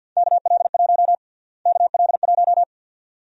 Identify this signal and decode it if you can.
Morse code, heard as a keyed tone, about 52 wpm, 700 Hz. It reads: KL1 KL1
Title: USA Call Sign Prefixes - Repeated 2x - 50wpm